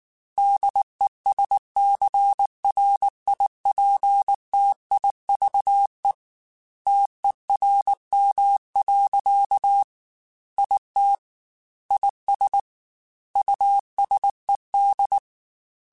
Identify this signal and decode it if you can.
Morse code, heard as a keyed tone, about 19 wpm, 800 Hz. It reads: DESCRIPTIVE TERM. IT IS USED
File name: Technical Subjects / Technical Subjects-mp3s / Reactance @18